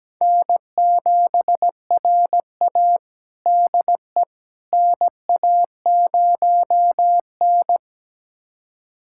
Morse code, heard as a keyed tone, about 17 wpm, 700 Hz. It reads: N7RA DE NA0N
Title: USA Callsign DE Callsign - 4 Characters - 17wpm